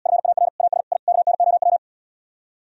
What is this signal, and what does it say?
Morse code, heard as a keyed tone, about 77 wpm, 700 Hz. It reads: FAR AS I KNOW